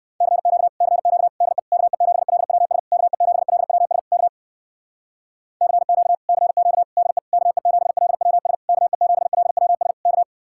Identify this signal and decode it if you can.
Morse code, heard as a keyed tone, about 59 wpm, 700 Hz. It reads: CQ CQ DE KE7LWS KE7LWS K CQ CQ DE KE7LWS KE7LWS K